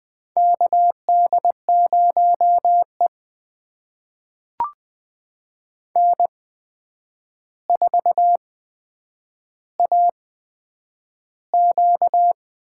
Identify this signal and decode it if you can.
Morse code, heard as a keyed tone, about 21 wpm, spaced as for 9 wpm, 700 Hz. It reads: KD0E N4AQ
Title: USA Call Signs - 4 Characters - 8x Character Spacing - 20wpm